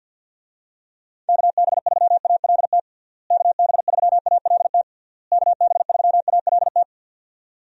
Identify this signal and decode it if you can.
Morse code, heard as a keyed tone, about 51 wpm, 700 Hz. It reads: KB3ALT KB3ALT KB3ALT
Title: USA Call Signs - 6 Characters - Repeated 3x - 50wpm